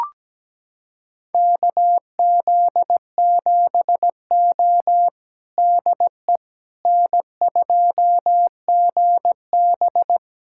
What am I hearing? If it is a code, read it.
Morse code, heard as a keyed tone, about 17 wpm, 700 Hz. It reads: KZ7O DE N2GB
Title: USA Callsign DE Callsign - 4 Characters - 17wpm